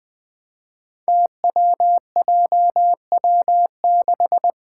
Morse code, heard as a keyed tone, about 20 wpm, 700 Hz. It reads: TWJW6
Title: Five Challenging Letter-Number Combinations 20wpm